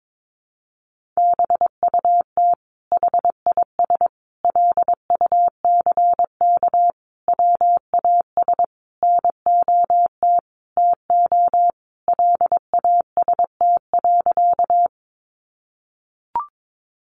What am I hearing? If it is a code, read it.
Morse code, heard as a keyed tone, about 22 wpm, 700 Hz. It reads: BUT HIS LUCK WAS NOT TO LAST.
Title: Sentences from Top 1500 Words 22wpm